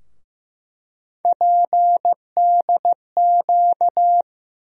Morse code, heard as a keyed tone, about 15 wpm, 700 Hz. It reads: PDQ
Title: USA Call Sign Postfixes - 3 Letters 15wpm